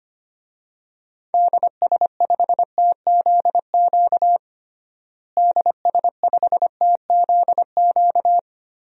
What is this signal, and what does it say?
Morse code, heard as a keyed tone, about 25 wpm, 700 Hz. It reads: DS5TZQ DS5TZQ
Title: DXCC-100 Call Signs - 6 Characters - Repeated 3x - 25wpm